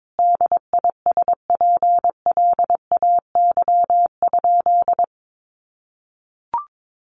Morse code, heard as a keyed tone, about 22 wpm, 700 Hz. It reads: DISPLAY?
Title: Sentences from Top 1200 Words 22wpm